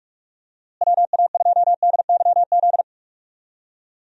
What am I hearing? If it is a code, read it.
Morse code, heard as a keyed tone, about 46 wpm, 700 Hz. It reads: WA2DYZ
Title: USA Call Signs - 6 Characters - Repeated 3x - 45wpm